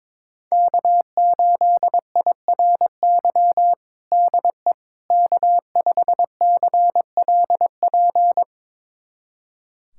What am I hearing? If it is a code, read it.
Morse code, heard as a keyed tone, about 22 wpm, 700 Hz. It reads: K8IRY DE K5CLP
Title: USA Callsign DE Callsign - 5 Characters - 22wpm